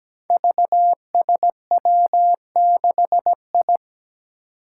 Morse code, heard as a keyed tone, about 17 wpm, 700 Hz. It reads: VSW6I
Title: Five Challenging Letter-Number Combinations 17wpm